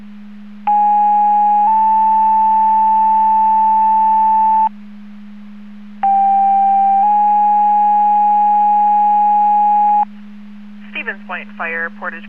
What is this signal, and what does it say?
Two-tone page, heard as a one-second tone, then a three-second tone, about 830 then 880 Hz, about 790 then 830 Hz. It is heard from 0.7 to 10.0 s.